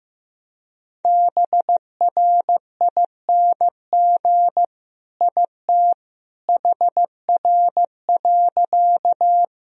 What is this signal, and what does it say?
Morse code, heard as a keyed tone, about 15 wpm, 700 Hz. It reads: BRING IT HR.
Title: Sentences from Top 500 Words and Abbreviations 15wpm